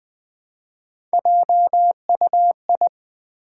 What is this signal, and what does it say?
Morse code, heard as a keyed tone, about 21 wpm, 700 Hz. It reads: JUI
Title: Build A Word - 1 Letter At A Time - 3 to 5-Letter Words - 20wpm